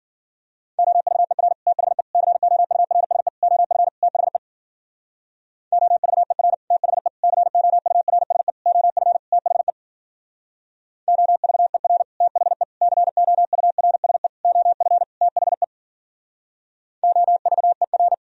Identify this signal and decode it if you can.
Morse code, heard as a keyed tone, about 52 wpm, 700 Hz. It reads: OVER THE COURSE OF THE OVER THE COURSE OF THE OVER THE COURSE OF THE OVER